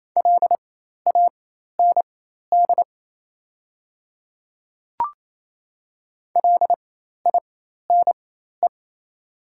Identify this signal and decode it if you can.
Morse code, heard as a keyed tone, about 27 wpm, spaced as for 13 wpm, 700 Hz. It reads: LAND LINE
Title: 4x Character Spacing - Top 200 Words - 28wpm